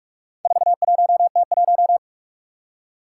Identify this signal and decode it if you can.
Morse code, heard as a keyed tone, about 44 wpm, 700 Hz. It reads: 41T1
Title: Four Challenging Letter-Number Combinations 45wpm